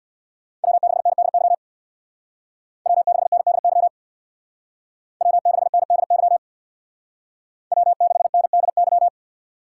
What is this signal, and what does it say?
Morse code, heard as a keyed tone, about 64 wpm, 700 Hz. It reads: W6NDY W6NDY W6NDY W6NDY